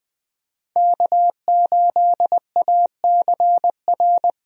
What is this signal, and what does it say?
Morse code, heard as a keyed tone, about 20 wpm, 700 Hz. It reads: K8ACR
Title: USA Call Signs - 5 Characters - Repeated 2x - 20wpm